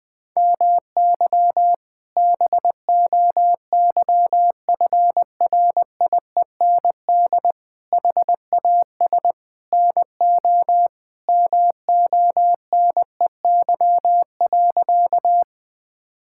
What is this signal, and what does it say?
Morse code, heard as a keyed tone, about 20 wpm, 700 Hz. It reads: MY BOYFRIEND HAS NO MONEY.
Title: Sentences from Top 1000 Words - Review Part 5 - 20wpm